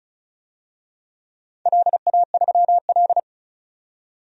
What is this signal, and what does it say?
Morse code, heard as a keyed tone, about 36 wpm, 700 Hz. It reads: LA3L
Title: DXCC-100 Call Signs - 4 Characters - 35wpm